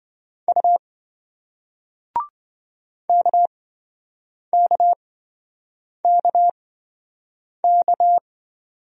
Morse code, heard as a keyed tone, about 26 wpm, spaced as for 7 wpm, 700 Hz. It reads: U KKKK